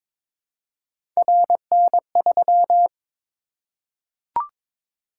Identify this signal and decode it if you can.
Morse code, heard as a keyed tone, about 22 wpm, 700 Hz. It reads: RN3
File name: practice / 22wpm / Three Letter-Number Combinations - 22wpm